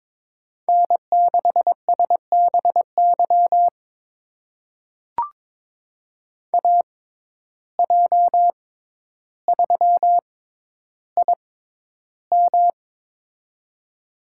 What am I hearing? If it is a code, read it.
Morse code, heard as a keyed tone, about 22 wpm, spaced as for 11 wpm, 700 Hz. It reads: N6SBY AJ3IM
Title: USA Call Signs - 5 Characters - 6x Character Spacing - 22wpm